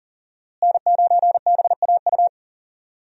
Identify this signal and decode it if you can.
Morse code, heard as a keyed tone, about 40 wpm, 700 Hz. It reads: N9BAU
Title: USA Call Signs - 5 Characters - Repeated 3x - 40wpm